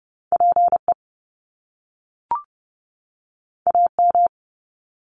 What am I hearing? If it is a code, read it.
Morse code, heard as a keyed tone, about 30 wpm, 700 Hz. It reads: PE AM
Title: Top Two Letter Combinations in Words 30wpm